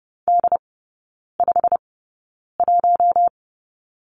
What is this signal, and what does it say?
Morse code, heard as a keyed tone, about 30 wpm, spaced as for 9 wpm, 700 Hz. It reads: D51
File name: course / 30:10wpm / 087 Lesson 15e - Number 5 - Three Characters 30wpm